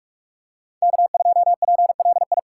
Morse code, heard as a keyed tone, about 42 wpm, 700 Hz. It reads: K2PLI
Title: USA Call Signs - 5 Characters - 4x Character Spacing - 45wpm